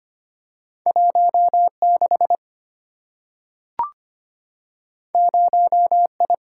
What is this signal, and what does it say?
Morse code, heard as a keyed tone, about 25 wpm, 700 Hz. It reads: 16 0I
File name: Two Letter-Number Combinations - 25wpm